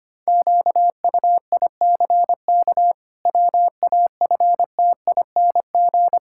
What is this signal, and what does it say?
Morse code, heard as a keyed tone, about 25 wpm, 700 Hz. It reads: QUICK WAFTING
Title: Pangram Challenge v2 - 25wpm